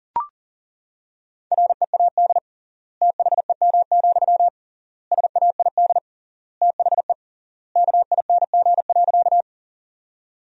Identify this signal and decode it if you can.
Morse code, heard as a keyed tone, about 40 wpm, 700 Hz. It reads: READ THEM, SAID THE KING.
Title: 3x Word-Spacing - Sentences from Top 400 Words - 40wpm